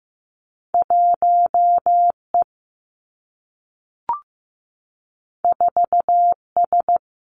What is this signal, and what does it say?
Morse code, heard as a keyed tone, about 15 wpm, 700 Hz. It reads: 1 E 4 S
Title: Two Challenging Letter-Number Combinations 15wpm